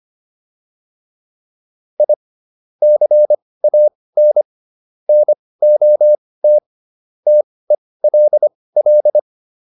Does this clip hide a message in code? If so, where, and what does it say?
Yes: Morse code from 2.0 to 9.2 s, heard as a keyed tone, about 25 wpm, spaced as for 18 wpm, 600 Hz. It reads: I CAN NOT TELL